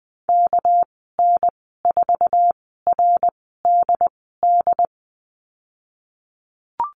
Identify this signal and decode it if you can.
Morse code, heard as a keyed tone, about 20 wpm, spaced as for 15 wpm, 700 Hz. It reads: KN4RDD
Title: 213 Lesson 34g - Number 8 - Call Signs 20wpm